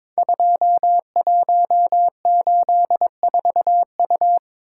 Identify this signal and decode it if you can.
Morse code, heard as a keyed tone, about 22 wpm, 700 Hz. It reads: 2184U
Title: Code Groups - 5 Letter:Numbers - 22wpm